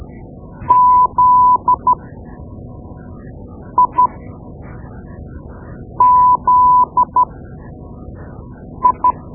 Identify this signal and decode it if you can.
Morse code, heard as a keyed tone, about 12 wpm, spaced as for 4 wpm, 1000 Hz. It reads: ZIZI
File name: Morse Lessons / Morse Lessons / Tape-2B